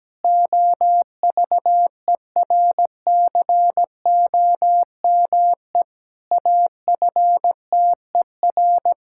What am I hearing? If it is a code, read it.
Morse code, heard as a keyed tone, about 17 wpm, 700 Hz. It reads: OVERCOME AFTER
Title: Pangram Challenge v2 - 17wpm